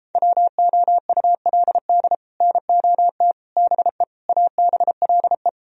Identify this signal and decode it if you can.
Morse code, heard as a keyed tone, about 33 wpm, 700 Hz. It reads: WOULD NOT BE ABLE